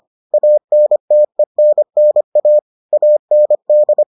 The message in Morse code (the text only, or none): ANTENNA AND